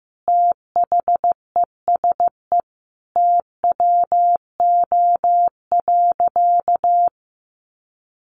Morse code, heard as a keyed tone, about 14 wpm, 700 Hz. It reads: THESE TWO.
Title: Sentences from Top 1200 Words 15wpm